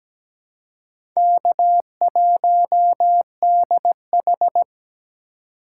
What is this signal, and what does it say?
Morse code, heard as a keyed tone, about 17 wpm, 700 Hz. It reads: K1DH
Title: USA Call Signs - 4 Characters - 4x Character Spacing - 17wpm